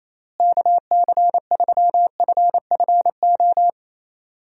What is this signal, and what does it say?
Morse code, heard as a keyed tone, about 28 wpm, 700 Hz. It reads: KC3FFO
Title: USA Call Signs - 6 Characters - Repeated 3x - 28wpm